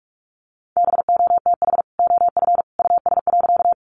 Morse code, heard as a keyed tone, about 44 wpm, 700 Hz. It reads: BOTH OF US.
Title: Sentences from Top 300 Words 45wpm